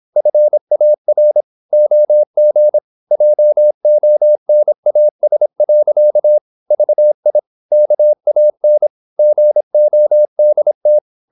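Morse code, heard as a keyed tone, about 26 wpm, 600 Hz. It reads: FAR OG JONAS. VI KAN GODT
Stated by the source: High-speed (QRQ) Morse at 26 wpm.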